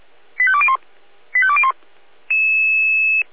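Five-tone selective call (ZVEI), heard as quick runs of five tones, tones of about 70 ms. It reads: 85201 85201
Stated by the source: ZVEI five-tone selective call 85894 received on SDR.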